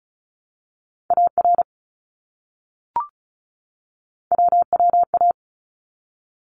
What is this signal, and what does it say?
Morse code, heard as a keyed tone, about 36 wpm, 700 Hz. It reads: AR WWA
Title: ARRL Sections 35wpm